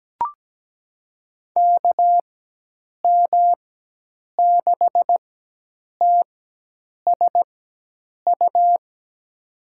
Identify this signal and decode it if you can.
Morse code, heard as a keyed tone, about 17 wpm, spaced as for 8 wpm, 700 Hz. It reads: KM6TSU
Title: USA Call Signs - 6 Characters - 4x Character Spacing - 17wpm